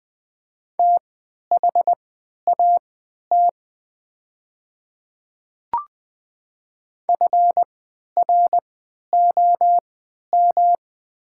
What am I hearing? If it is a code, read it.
Morse code, heard as a keyed tone, about 20 wpm, spaced as for 11 wpm, 700 Hz. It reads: THAT FROM
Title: 3x Character Spacing - Top 200 Words - 20wpm